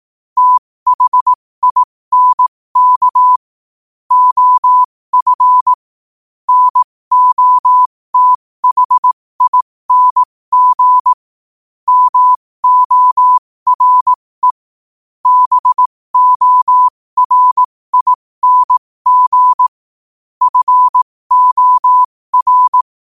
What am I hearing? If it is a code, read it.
Morse code, heard as a keyed tone, about 18 wpm, spaced as for 15 wpm, 1000 Hz. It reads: THINK OF NOTHING MORE BORING FOR